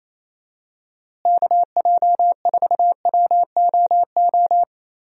Morse code, heard as a keyed tone, about 28 wpm, 700 Hz. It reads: KJ4WOO
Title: USA Call Signs - 6 Characters - Repeated 3x - 28wpm